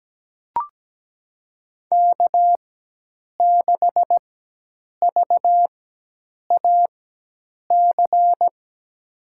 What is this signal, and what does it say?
Morse code, heard as a keyed tone, about 17 wpm, spaced as for 8 wpm, 700 Hz. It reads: K6VAC
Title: USA Call Signs - 4x Character Spacing - 17wpm